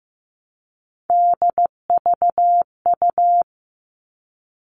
Morse code, heard as a keyed tone, about 15 wpm, 700 Hz. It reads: DVU
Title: All Cut Numbers - 3-Digit 15wpm